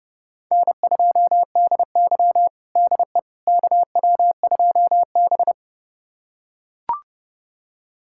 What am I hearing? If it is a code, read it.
Morse code, heard as a keyed tone, about 30 wpm, 700 Hz. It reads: N2DY DE KW2B
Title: USA Callsign DE Callsign - 4 Characters - 30wpm